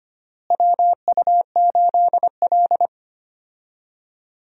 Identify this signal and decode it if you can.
Morse code, heard as a keyed tone, about 25 wpm, 700 Hz. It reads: WU8L
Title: USA Call Signs - 4 Characters - Repeated 3x - 25wpm